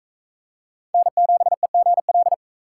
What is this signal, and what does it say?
Morse code, heard as a keyed tone, about 41 wpm, 700 Hz. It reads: N7EGL